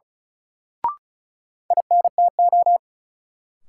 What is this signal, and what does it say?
Morse code, heard as a keyed tone, about 36 wpm, 700 Hz. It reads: INTO
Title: Sentences from Top 600 Words - New Word Drill - 35wpm